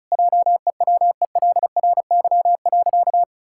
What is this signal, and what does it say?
Morse code, heard as a keyed tone, about 34 wpm, 700 Hz. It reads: JEWELRY.
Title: Pangram Challenge v2 - 35wpm